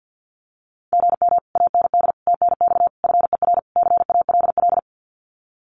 Morse code, heard as a keyed tone, about 50 wpm, 700 Hz. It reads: GM AND TNX FER CALL
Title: Common QSO Phrases Encore 50wpm